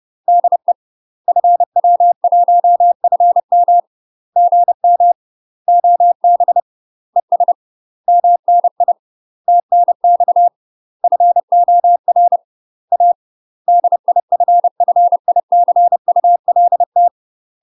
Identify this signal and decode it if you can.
Morse code, heard as a keyed tone, about 30 wpm, 700 Hz. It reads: DE FW1FM GM OB ES MNI TNX FOR A DIFFICULT
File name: UK MASTER QSOs TRAINING / 30WPM / UK MASTER QSOs 0023